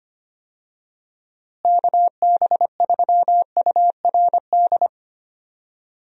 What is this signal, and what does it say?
Morse code, heard as a keyed tone, about 25 wpm, 700 Hz. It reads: KB3URD
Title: USA Call Signs - 6 Characters - Repeated 3x - 25wpm